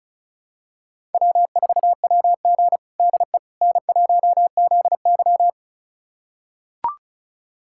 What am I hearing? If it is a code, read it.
Morse code, heard as a keyed tone, about 36 wpm, 700 Hz. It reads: W4WG DE N1ZY
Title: USA Callsign DE Callsign - 4 Characters - 35wpm